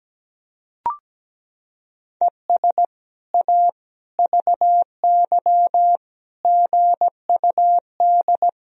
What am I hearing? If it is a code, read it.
Morse code, heard as a keyed tone, about 17 wpm, 700 Hz. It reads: ES A VY GUD
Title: Sentences from Top 200 Words and Abbreviations 17wpm